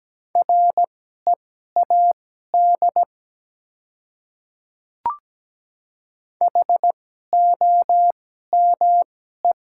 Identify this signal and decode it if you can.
Morse code, heard as a keyed tone, about 17 wpm, spaced as for 12 wpm, 700 Hz. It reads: READ HOME
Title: 2x Character Spacing - Top 200 Words - 17wpm